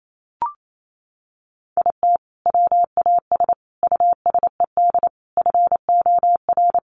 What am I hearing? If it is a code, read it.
Morse code, heard as a keyed tone, about 28 wpm, 700 Hz. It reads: IT WAS USED FOR
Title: Sentences from Top 1400 Words 28wpm